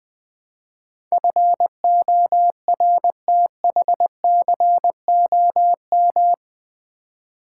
Morse code, heard as a keyed tone, about 20 wpm, 700 Hz. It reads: FORTHCOM
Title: Build A Word - 4 Letters At A Time - 11-Letter Words - 20wpm